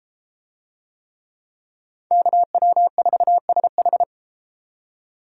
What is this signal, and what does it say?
Morse code, heard as a keyed tone, about 32 wpm, 700 Hz. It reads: KW4SH